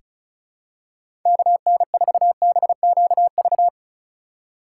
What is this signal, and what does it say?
Morse code, heard as a keyed tone, about 36 wpm, 700 Hz. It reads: KN4BQV
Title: USA Call Signs - 6 Characters - 4x Character Spacing - 35wpm